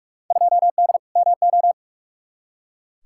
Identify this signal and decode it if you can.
Morse code, heard as a keyed tone, about 44 wpm, 700 Hz. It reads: 2D MO
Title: Field Day Exchanges - 45wpm